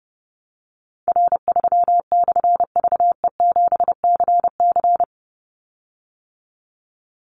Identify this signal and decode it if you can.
Morse code, heard as a keyed tone, about 30 wpm, 700 Hz. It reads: R3/VE7CC
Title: Call Signs - 30wpm